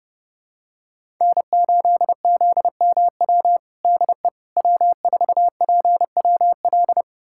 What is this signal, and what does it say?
Morse code, heard as a keyed tone, about 30 wpm, 700 Hz. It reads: N8ZMW DE W4PWL
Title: USA Callsign DE Callsign - 5 Characters - 30wpm